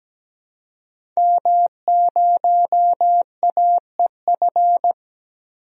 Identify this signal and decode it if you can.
Morse code, heard as a keyed tone, about 17 wpm, 700 Hz. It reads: M0AEF
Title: DXCC-100 Call Signs - 5 Characters - Repeated 2x - 17wpm